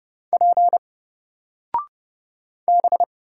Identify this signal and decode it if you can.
Morse code, heard as a keyed tone, about 31 wpm, 700 Hz. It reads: P B
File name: Single-Letter-Number-30wpm